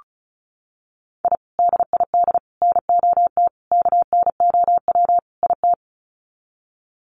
Morse code, heard as a keyed tone, about 34 wpm, 700 Hz. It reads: I DID NOT KNOW IT
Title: Sets of 5 Words Encore2 35wpm